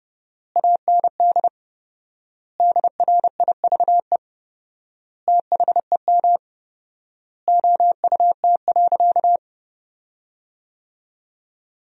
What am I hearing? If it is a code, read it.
Morse code, heard as a keyed tone, about 30 wpm, 700 Hz. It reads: AND DRIVE THEM OUT.
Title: 4x Word-Spacing - Sentences from Top 500 Words - 30wpm